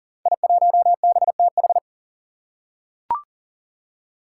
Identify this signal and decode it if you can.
Morse code, heard as a keyed tone, about 39 wpm, 700 Hz. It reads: I1BTH
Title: Five Challenging Letter-Number Combinations 40wpm